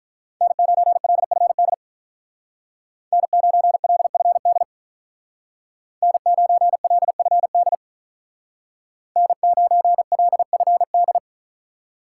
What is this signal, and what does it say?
Morse code, heard as a keyed tone, about 43 wpm, 700 Hz. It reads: N9LFD N9LFD N9LFD N9LFD
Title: Speed Racing - USA 1x3 Call Signs 35wpm